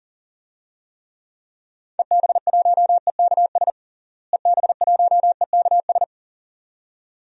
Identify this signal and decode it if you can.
Morse code, heard as a keyed tone, about 40 wpm, 700 Hz. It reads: EB1EKS EB1EKS
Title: DXCC-100 Call Signs - 6 Characters - Repeated 2x - 40wpm